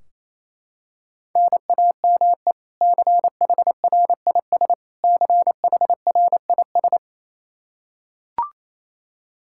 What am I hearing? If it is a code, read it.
Morse code, heard as a keyed tone, about 27 wpm, 700 Hz. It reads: NAME CHRIS CHRIS
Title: Practice QSO - Exchange 1 - Beginning - 28wpm